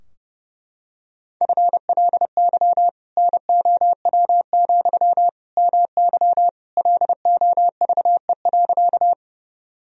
Morse code, heard as a keyed tone, about 30 wpm, 700 Hz. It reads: FLY NOW, MY LOVE.
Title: Sentences from Top 1000 Words - Review Part 2 - 30wpm